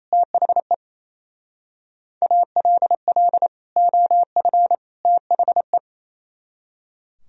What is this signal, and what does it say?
Morse code, heard as a keyed tone, about 34 wpm, 700 Hz. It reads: THE ALL OF THE
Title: Speed-Racing - Sets of 3 Words 28wpm